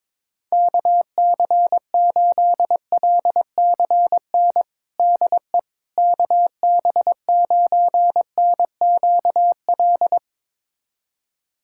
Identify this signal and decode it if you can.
Morse code, heard as a keyed tone, about 21 wpm, 700 Hz. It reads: KC8LCN DE KB9NQL
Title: USA Callsign DE Callsign - 6 Characters - 22wpm